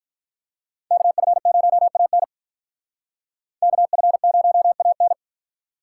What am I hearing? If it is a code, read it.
Morse code, heard as a keyed tone, about 45 wpm, 700 Hz. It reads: KF0AN KF0AN